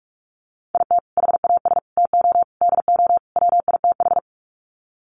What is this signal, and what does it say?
Morse code, heard as a keyed tone, about 44 wpm, 700 Hz. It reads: IT HAS TO DO WITH
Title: Sets of 5 Words 45wpm